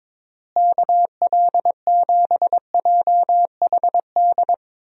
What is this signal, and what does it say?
Morse code, heard as a keyed tone, about 22 wpm, 700 Hz. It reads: KL7JHD
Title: USA Call Signs - 6 Characters - Repeated 3x - 22wpm